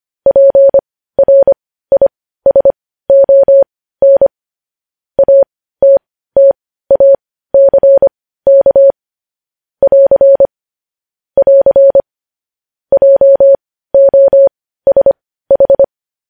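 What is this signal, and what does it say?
Morse code, heard as a keyed tone, about 25 wpm, spaced as for 15 wpm, 550 Hz. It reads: PRISON ATTACK <AR> <AR> JOSH
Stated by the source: News headlines sent in Morse code at 15 wpm.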